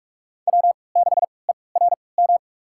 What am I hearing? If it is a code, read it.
Morse code, heard as a keyed tone, about 44 wpm, spaced as for 25 wpm, 700 Hz. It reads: W6ERM
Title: USA Call Signs - 5 Characters - 3x Character Spacing - 45wpm